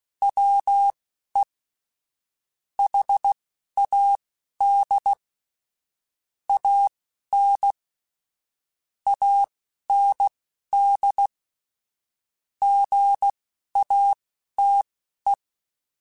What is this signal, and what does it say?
Morse code, heard as a keyed tone, about 16 wpm, spaced as for 11 wpm, 800 Hz. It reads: WE HAD AN AND GATE
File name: Technical Subjects / Technical Subjects-mp3s / AND Gate @10